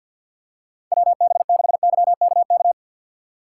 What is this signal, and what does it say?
Morse code, heard as a keyed tone, about 46 wpm, 700 Hz. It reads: WB6YKK
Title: USA Call Signs - 2x Character Spacing - 50wpm